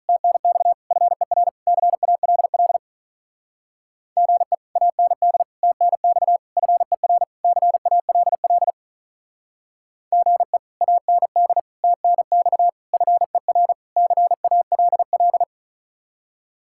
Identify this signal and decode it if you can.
Morse code, heard as a keyed tone, about 46 wpm, 700 Hz. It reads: TNX FER CALL GE AND TNX FER CALL GE AND TNX FER CALL